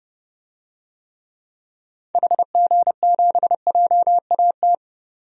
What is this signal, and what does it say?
Morse code, heard as a keyed tone, about 30 wpm, 700 Hz. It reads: HG7JAT
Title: DXCC-100 Call Signs - 6 Characters - Repeated 2x - 30wpm